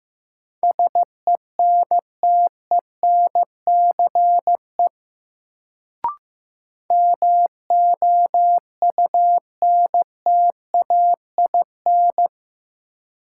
Morse code, heard as a keyed tone, about 15 wpm, 700 Hz. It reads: SENTENCE MOUNTAIN
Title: Sentences from Top 1000 Words - Review Part 1 - New Word Drill - 15wpm